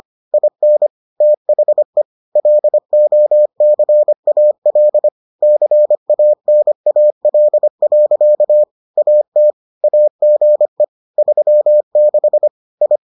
Morse code, heard as a keyed tone, about 25 wpm, 600 Hz. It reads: IN THE LOCAL CANAL. AT AGE 36 I